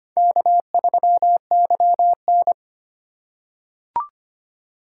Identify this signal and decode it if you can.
Morse code, heard as a keyed tone, about 25 wpm, 700 Hz. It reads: K3YN
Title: USA Call Signs - 4 Characters - 3x Character Spacing - 25wpm